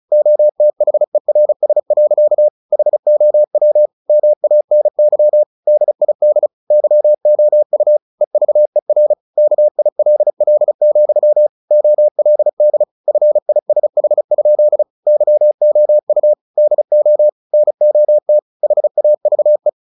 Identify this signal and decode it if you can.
Morse code, heard as a keyed tone, about 36 wpm, 600 Hz. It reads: OTHERS. HOW MANY DID YOU EVER KILL, OLD FISH? YOU DO NOT HAVE
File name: the old man and the sea 2 35wpm